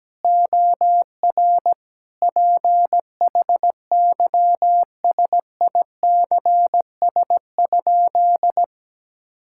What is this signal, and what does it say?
Morse code, heard as a keyed tone, about 17 wpm, 700 Hz. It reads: OR PHYSICS?
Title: Sentences from Top 1000 Words - Review Part 6 - 17wpm